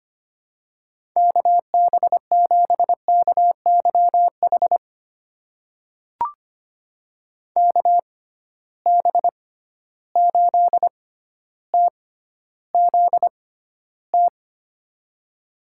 Morse code, heard as a keyed tone, about 25 wpm, spaced as for 13 wpm, 700 Hz. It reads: KB7KYH KB8TZT
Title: USA Call Signs - 6 Characters - 6x Character Spacing - 25wpm